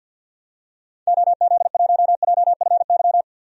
Morse code, heard as a keyed tone, about 52 wpm, 700 Hz. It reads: OZ1JFY
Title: DXCC-100 Call Signs - 6 Characters - 50wpm